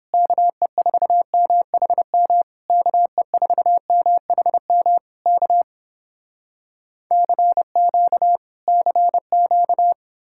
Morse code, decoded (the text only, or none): KE4MHM KE4MHM K CQ CQ